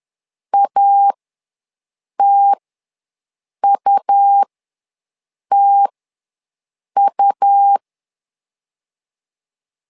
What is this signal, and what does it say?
Morse code, heard as a keyed tone, about 11 wpm, spaced as for 6 wpm, 800 Hz. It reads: ATUTU
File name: LEZIONI / 2 UHDN / 2 U ws 10wpm 2017